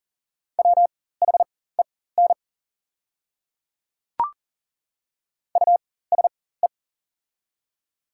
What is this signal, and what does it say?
Morse code, heard as a keyed tone, about 40 wpm, spaced as for 19 wpm, 690 Hz. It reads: WHEN USE
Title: 4x Character Spacing - Top 100 Words - 40wpm